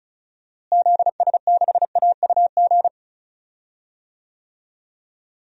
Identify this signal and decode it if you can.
Morse code, heard as a keyed tone, about 34 wpm, 700 Hz. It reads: ZS6AUG
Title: DXCC-100 Call Signs - 6 Characters - Repeated 2x - 35wpm